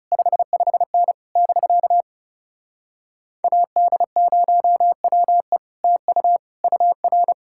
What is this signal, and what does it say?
Morse code, heard as a keyed tone, about 40 wpm, 700 Hz. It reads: 55N <BK> AD0WE TU UR